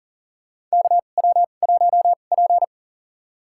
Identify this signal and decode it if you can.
Morse code, heard as a keyed tone, about 40 wpm, spaced as for 29 wpm, 700 Hz. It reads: KW1P
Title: USA Call Signs - 4 Characters - 2x Character Spacing - 40wpm